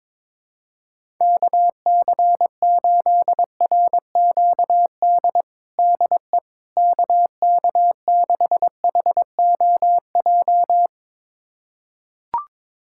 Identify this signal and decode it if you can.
Morse code, heard as a keyed tone, about 21 wpm, 700 Hz. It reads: KC8RQD DE KK6HOJ
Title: USA Callsign DE Callsign - 6 Characters - 22wpm